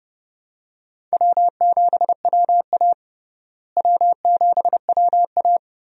Morse code, heard as a keyed tone, about 30 wpm, 700 Hz. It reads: W7WA W7WA
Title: USA Call Signs - 4 Characters - Repeated 3x - 30wpm